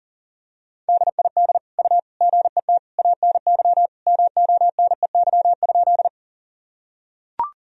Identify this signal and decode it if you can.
Morse code, heard as a keyed tone, about 40 wpm, 700 Hz. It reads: DID U GET ANY MONEY?
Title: Sentences from Top 400 Words and Abbreviations 40wpm